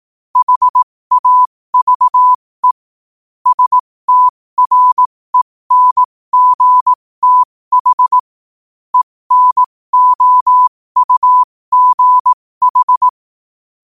Morse code, heard as a keyed tone, about 18 wpm, spaced as for 15 wpm, 1000 Hz. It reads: HAVE STRENGTH ENOUGH
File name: qotdfiles / 15wpm